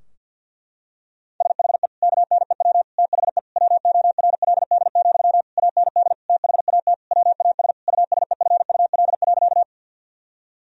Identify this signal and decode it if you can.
Morse code, heard as a keyed tone, about 50 wpm, 700 Hz. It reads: SHE KNEW THE WORLD, AND THAT WAS USEFUL.
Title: Sentences from Top 1000 Words - Review Part 4 - 50wpm